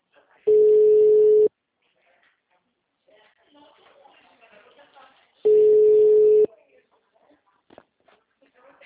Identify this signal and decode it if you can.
Ringback tone (Europe, 425 Hz), with speech.
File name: out- -2002- - - .86587